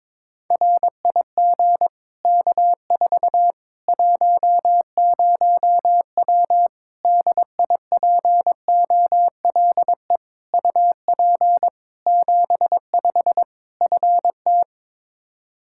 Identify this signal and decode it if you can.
Morse code, heard as a keyed tone, about 22 wpm, 700 Hz. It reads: RIG K4 10W DIPOLE UP 75 FT
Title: Practice QSO - Exchange 2 - Info - 22wpm